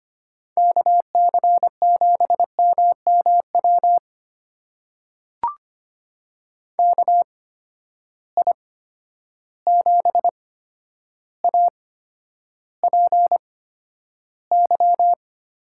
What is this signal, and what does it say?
Morse code, heard as a keyed tone, about 25 wpm, spaced as for 11 wpm, 700 Hz. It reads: KC7MMW KI7APY
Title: USA Call Signs - 6 Characters - 8x Character Spacing - 25wpm